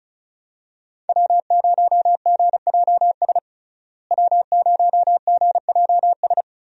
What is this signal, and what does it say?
Morse code, heard as a keyed tone, about 36 wpm, 700 Hz. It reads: W0GJS W0GJS
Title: USA Call Signs - 5 Characters - Repeated 3x - 35wpm